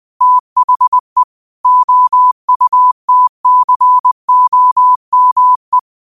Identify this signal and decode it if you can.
Morse code, heard as a keyed tone, about 20 wpm, 1000 Hz. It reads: THE OUTCOME
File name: qotdfiles / 20wpm